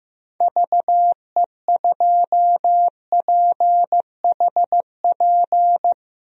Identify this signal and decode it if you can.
Morse code, heard as a keyed tone, about 15 wpm, 700 Hz. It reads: VE2PHP
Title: DXCC-100 Call Signs - 6 Characters - Repeated 2x - 15wpm